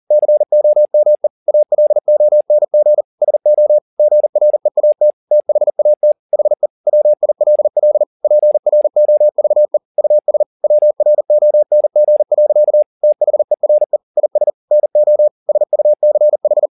Morse code, heard as a keyed tone, about 40 wpm, 600 Hz. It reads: COME ALONG SO GREAT THAT HE WILL PROVE US WRONG. THERE IS NO SUCH